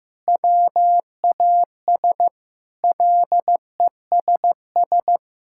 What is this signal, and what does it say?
Morse code, heard as a keyed tone, about 15 wpm, 700 Hz. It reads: WAS LESS
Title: Sentences from Top 1600 Words 15wpm